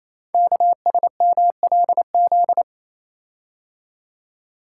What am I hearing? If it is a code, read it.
Morse code, heard as a keyed tone, about 28 wpm, 700 Hz. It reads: KSMLZ
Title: Code Groups - 5 Letters - 28wpm